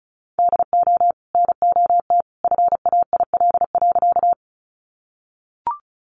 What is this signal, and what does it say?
Morse code, heard as a keyed tone, about 34 wpm, 700 Hz. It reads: DO NOT FAIL.
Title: Sentences from Top 1100 Words 35wpm